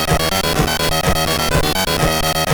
There is no code.